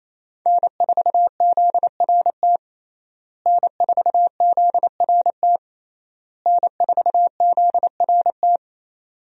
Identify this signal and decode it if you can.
Morse code, heard as a keyed tone, about 28 wpm, 700 Hz. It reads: N4ZRT N4ZRT N4ZRT
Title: USA Call Signs - 5 Characters - Repeated 3x - 28wpm